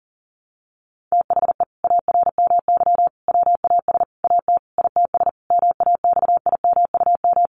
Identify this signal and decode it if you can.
Morse code, heard as a keyed tone, about 40 wpm, 700 Hz. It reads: THE ARMY WAS AT ITS MAXIMUM
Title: Sentences from Top 1300 Words 40wpm